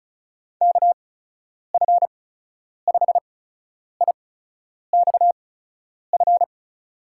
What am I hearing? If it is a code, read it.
Morse code, heard as a keyed tone, about 36 wpm, spaced as for 10 wpm, 700 Hz. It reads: KF5IXF
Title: USA Call Signs - 6 Characters - 8x Character Spacing - 35wpm